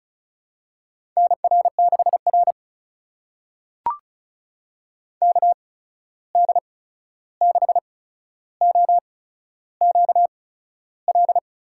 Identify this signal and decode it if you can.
Morse code, heard as a keyed tone, about 36 wpm, spaced as for 13 wpm, 700 Hz. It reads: NR6R KD6OQL
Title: USA Call Signs - 8x Character Spacing - 35wpm